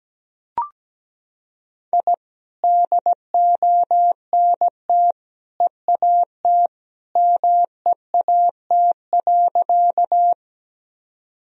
Morse code, heard as a keyed tone, about 17 wpm, 700 Hz. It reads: I DONT EAT MEAT.
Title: Sentences from Top 1000 Words - Review Part 3 - 17wpm